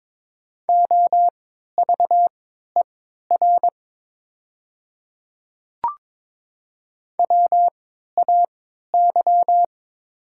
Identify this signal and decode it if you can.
Morse code, heard as a keyed tone, about 23 wpm, spaced as for 13 wpm, 700 Hz. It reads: OVER WAY
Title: 3x Character Spacing - Top 100 Words - 22wpm